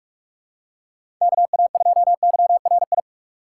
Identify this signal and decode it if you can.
Morse code, heard as a keyed tone, about 42 wpm, 700 Hz. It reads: KA2YRI
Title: USA Call Signs - 4x Character Spacing - 45wpm